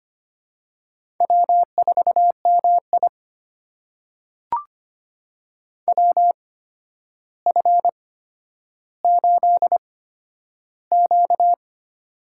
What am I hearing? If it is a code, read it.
Morse code, heard as a keyed tone, about 25 wpm, spaced as for 11 wpm, 700 Hz. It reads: W4MI WF8Q
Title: USA Call Signs - 8x Character Spacing - 25wpm